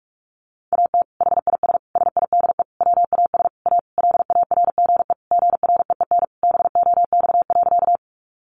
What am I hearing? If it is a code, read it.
Morse code, heard as a keyed tone, about 44 wpm, 700 Hz. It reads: AT HIS SIDE WAS A LARGE GREEN BOX.
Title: Sentences from Top 500 Words 45wpm